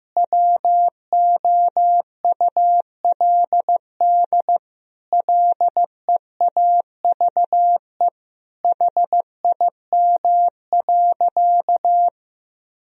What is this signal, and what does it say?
Morse code, heard as a keyed tone, about 15 wpm, 700 Hz. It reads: WOULD LEAVE HIM.
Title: Sentences from Top 1000 Words - Review Part 4 - 15wpm